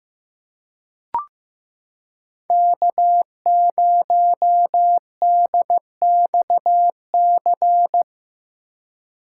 Morse code, heard as a keyed tone, about 15 wpm, 700 Hz. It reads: K0DXC
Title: K1USN SST QSO Practice - 15wpm